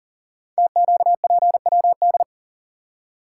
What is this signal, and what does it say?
Morse code, heard as a keyed tone, about 39 wpm, 700 Hz. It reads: TQPWD
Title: Code Groups - 5 Letters - 40wpm